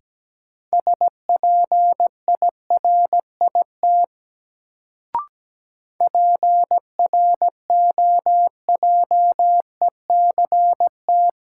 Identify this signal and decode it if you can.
Morse code, heard as a keyed tone, about 17 wpm, 700 Hz. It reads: SPIRIT PROJECT
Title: Sentences from Top 1400 Words - New Word Drill Alt - 17wpm